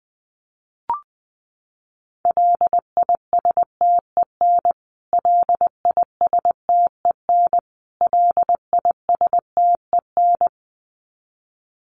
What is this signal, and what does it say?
Morse code, heard as a keyed tone, about 20 wpm, 700 Hz. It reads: LISTEN LISTEN LISTEN
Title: Top 500 Words Repeated 20wpm